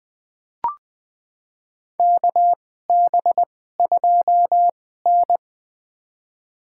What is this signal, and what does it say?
Morse code, heard as a keyed tone, about 20 wpm, spaced as for 15 wpm, 700 Hz. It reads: KB2N
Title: USA Call Signs - 4 Characters - 2x Character Spacing - 20wpm